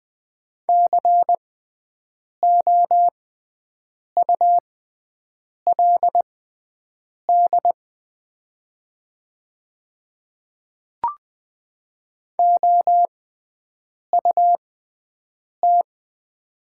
Morse code, heard as a keyed tone, about 20 wpm, spaced as for 7 wpm, 700 Hz. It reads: COULD OUT
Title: 6x Character Spacing - Top 200 Words - 20wpm